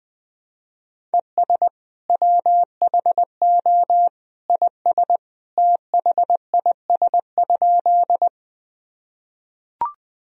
Morse code, heard as a keyed tone, about 20 wpm, 700 Hz. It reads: ES WHO IS THIS?
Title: Sentences from Top 100 Words and Abbreviations 20wpm